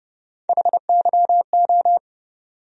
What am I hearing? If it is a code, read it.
Morse code, heard as a keyed tone, about 30 wpm, 700 Hz. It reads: HYO
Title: USA Call Sign Postfixes - 3 Letters 30wpm